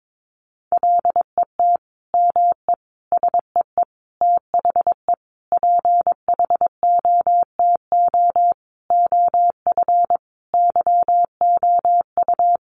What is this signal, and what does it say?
Morse code, heard as a keyed tone, about 22 wpm, 700 Hz. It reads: LET ME SEE THE PHOTO OF YOU
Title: Sentences from Top 1200 Words 22wpm